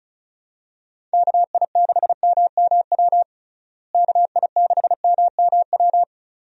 Morse code, heard as a keyed tone, about 34 wpm, 700 Hz. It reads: KI6MMW KI6MMW
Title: USA Call Signs - 6 Characters - Repeated 2x - 35wpm